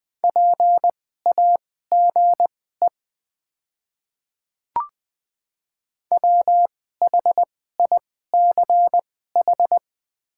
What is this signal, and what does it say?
Morse code, heard as a keyed tone, about 20 wpm, spaced as for 14 wpm, 700 Hz. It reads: PAGE WHICH
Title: 2x Character Spacing - Top 300 Words - 20wpm